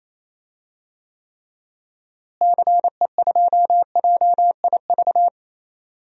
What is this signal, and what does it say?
Morse code, heard as a keyed tone, about 27 wpm, 700 Hz. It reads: CE2JIV
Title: DXCC-100 Call Signs - 6 Characters - Repeated 3x - 28wpm